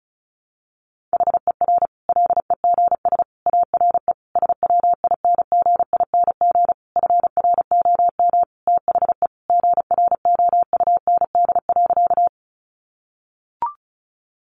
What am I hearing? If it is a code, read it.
Morse code, heard as a keyed tone, about 34 wpm, 700 Hz. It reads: HER LEGS ARE SWINGING FROM THE GROUND.
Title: Easy Phrases 35wpm